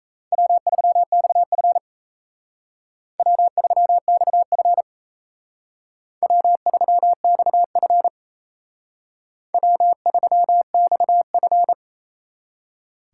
Morse code, heard as a keyed tone, about 36 wpm, 700 Hz. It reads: W3XF W3XF W3XF W3XF